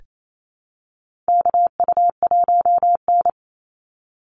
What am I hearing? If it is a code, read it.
Morse code, heard as a keyed tone, about 28 wpm, 700 Hz. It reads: KU1N
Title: Call Signs Repeated 3x - 28wpm